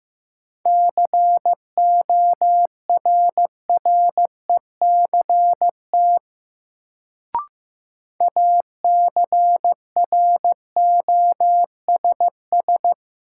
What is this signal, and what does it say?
Morse code, heard as a keyed tone, about 15 wpm, 700 Hz. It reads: CORRECT ACROSS
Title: Sentences from Top 700 Words - New Word Drill - 15wpm